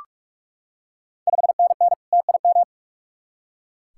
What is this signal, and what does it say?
Morse code, heard as a keyed tone, about 44 wpm, 690 Hz. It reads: 5NN TIM
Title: Ontario QSO Party Exchange 45wpm